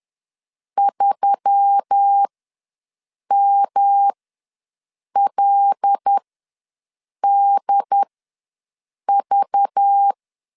Morse code, heard as a keyed tone, about 10 wpm, spaced as for 6 wpm, 800 Hz. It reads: 3MLDV
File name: LEZIONI / 7 GF9Z / 7 G ws 10wpm 3843